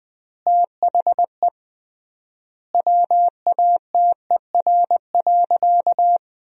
Morse code, heard as a keyed tone, about 20 wpm, 700 Hz. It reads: THE WATER.
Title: 3x Word-Spacing - Sentences from Top 300 Words - 20wpm